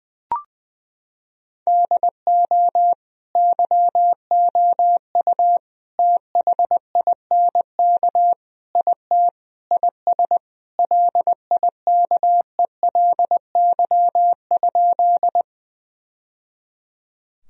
Morse code, heard as a keyed tone, about 20 wpm, 700 Hz. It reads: DO YOU THINK IT IS LIKELY?
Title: Sentences from Top 1000 Words - Review Part 4 - 20wpm